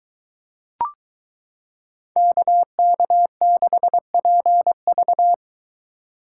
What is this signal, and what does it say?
Morse code, heard as a keyed tone, about 23 wpm, 700 Hz. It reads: KK6PV